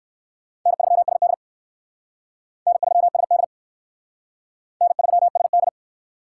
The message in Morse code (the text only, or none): N3SD N3SD N3SD